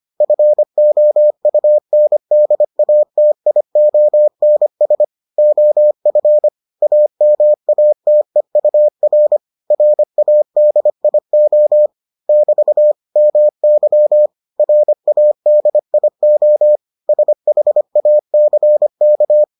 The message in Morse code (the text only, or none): FOUNDATIONS OF AMATEUR RADIO <BT> MY RADIO SHACK